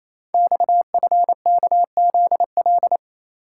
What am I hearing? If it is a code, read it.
Morse code, heard as a keyed tone, about 29 wpm, 700 Hz. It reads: XFKZL
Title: Code Groups - 5 Letters - 28wpm